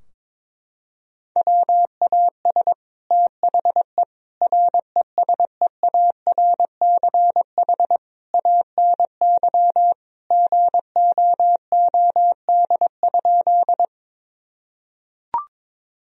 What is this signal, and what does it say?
Morse code, heard as a keyed tone, about 22 wpm, 700 Hz. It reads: WAS THE RESEARCH ANY GOOD?
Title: Sentences from Top 1000 Words - Review Part 4 - 22wpm